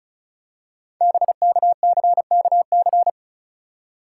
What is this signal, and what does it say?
Morse code, heard as a keyed tone, about 36 wpm, 700 Hz. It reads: BKCKC
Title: Code Groups - 5 Letters - 35wpm